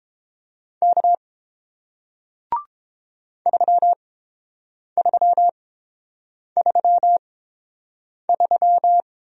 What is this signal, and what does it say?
Morse code, heard as a keyed tone, about 31 wpm, spaced as for 7 wpm, 700 Hz. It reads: K 3333